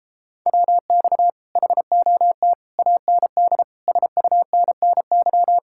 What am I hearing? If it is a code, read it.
Morse code, heard as a keyed tone, about 31 wpm, 700 Hz. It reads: WX HOT AND SUNNY